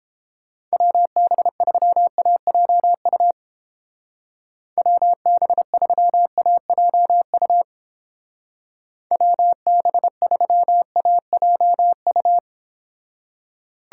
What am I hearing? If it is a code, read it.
Morse code, heard as a keyed tone, about 29 wpm, 700 Hz. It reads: WB3AJU WB3AJU WB3AJU